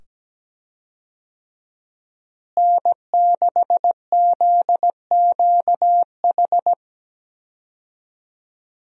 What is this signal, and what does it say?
Morse code, heard as a keyed tone, about 17 wpm, 700 Hz. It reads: N6ZQH
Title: USA Call Signs - 5 Characters - Repeated 2x - 17wpm